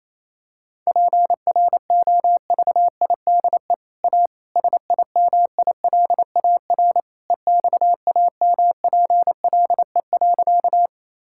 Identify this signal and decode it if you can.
Morse code, heard as a keyed tone, about 28 wpm, 700 Hz. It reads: PROVIDE A SIMILAR EXAMPLE.
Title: Sentences from Top 1000 Words - Review Part 5 - 28wpm